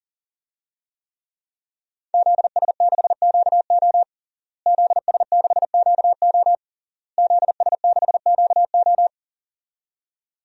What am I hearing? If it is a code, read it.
Morse code, heard as a keyed tone, about 37 wpm, 700 Hz. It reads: ZS6QO ZS6QO ZS6QO
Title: DXCC-100 Call Signs - 5 Characters - Repeated 3x - 40wpm